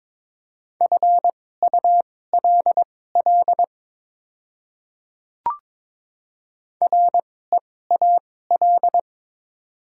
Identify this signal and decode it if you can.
Morse code, heard as a keyed tone, about 22 wpm, spaced as for 16 wpm, 700 Hz. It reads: FULL REAL
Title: 2x Character Spacing - Top 500 Words - 22wpm